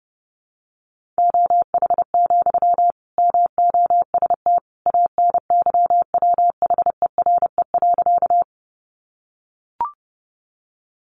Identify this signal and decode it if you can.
Morse code, heard as a keyed tone, about 30 wpm, 700 Hz. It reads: OH, MOST ANYWHERE.
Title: Sentences from Top 1400 Words 30wpm